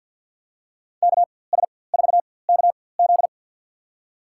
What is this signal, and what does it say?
Morse code, heard as a keyed tone, about 53 wpm, spaced as for 24 wpm, 700 Hz. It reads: KS4KZ
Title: USA Call Signs - 5 Characters - 4x Character Spacing - 50wpm